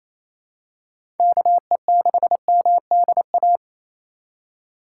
Morse code, heard as a keyed tone, about 28 wpm, 700 Hz. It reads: KE6MDA
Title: USA Call Signs - 6 Characters - 4x Character Spacing - 28wpm